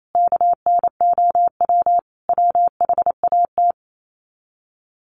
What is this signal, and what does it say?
Morse code, heard as a keyed tone, about 28 wpm, 700 Hz. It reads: KNOW WHAT
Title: Sets of 5 Words Encore 28wpm